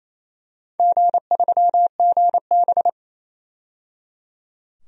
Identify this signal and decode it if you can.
Morse code, heard as a keyed tone, about 28 wpm, 700 Hz. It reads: G3GB
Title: DXCC-100 Call Signs - 4 Characters - 28wpm